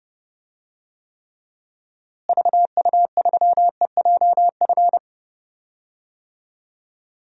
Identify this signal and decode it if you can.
Morse code, heard as a keyed tone, about 30 wpm, 700 Hz. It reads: VU3EJF
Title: DXCC-100 Call Signs - 6 Characters - Repeated 2x - 30wpm